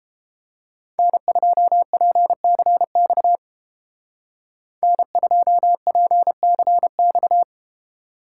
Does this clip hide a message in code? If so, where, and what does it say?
Yes: Morse code from 1.0 to 7.4 s, heard as a keyed tone, about 31 wpm, 700 Hz. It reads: N2PCX N2PCX